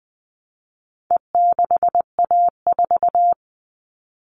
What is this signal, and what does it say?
Morse code, heard as a keyed tone, about 20 wpm, 700 Hz. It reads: E6A4
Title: All Cut Numbers - 4-Digit 20wpm